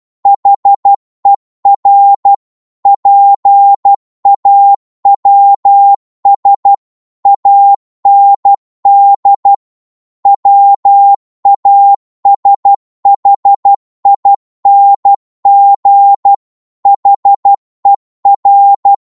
Morse code, heard as a keyed tone, about 12 wpm, 800 Hz. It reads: HER PAWS AND WASHING HER
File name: Morse / psg-12wpm